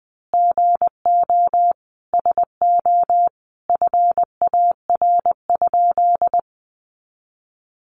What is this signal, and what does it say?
Morse code, heard as a keyed tone, about 20 wpm, 700 Hz. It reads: GO SO FAR?
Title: Sentences from Top 300 Words 20wpm